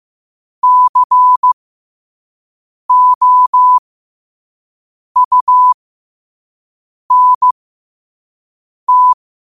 Morse code, heard as a keyed tone, about 15 wpm, spaced as for 5 wpm, 1000 Hz. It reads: COUNT
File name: qotdfiles / 5wpm